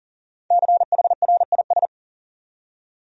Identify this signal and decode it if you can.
Morse code, heard as a keyed tone, about 40 wpm, 700 Hz. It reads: CHRIS
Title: USA Common Names - 40wpm